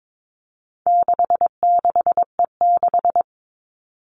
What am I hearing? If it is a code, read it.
Morse code, heard as a keyed tone, about 22 wpm, 700 Hz. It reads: 66E6
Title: All Cut Numbers - 4-Digit 22wpm